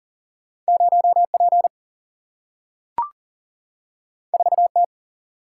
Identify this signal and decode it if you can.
Morse code, heard as a keyed tone, about 40 wpm, 700 Hz. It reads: 0P 4T
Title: Single-Letter Single-Number 40wpm